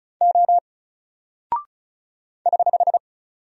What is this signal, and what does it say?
Morse code, heard as a keyed tone, about 36 wpm, 700 Hz. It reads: O <HH>
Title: Single Character (Letter:Number:Punctuation:Prosign) 35wpm